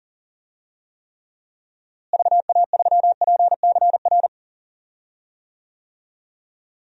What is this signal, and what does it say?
Morse code, heard as a keyed tone, about 40 wpm, 700 Hz. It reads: VA3PCR
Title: DXCC-100 Call Signs - 6 Characters - Repeated 3x - 40wpm